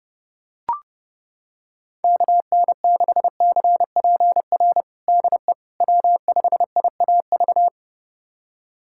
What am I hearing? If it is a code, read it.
Morse code, heard as a keyed tone, about 30 wpm, 700 Hz. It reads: KN6CPR DE W5IAV
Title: USA Callsign DE Callsign 30wpm